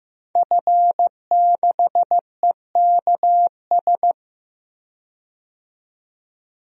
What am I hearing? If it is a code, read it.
Morse code, heard as a keyed tone, about 15 wpm, 700 Hz. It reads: F6EKS
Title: DXCC-100 Call Signs - 5 Characters - 15wpm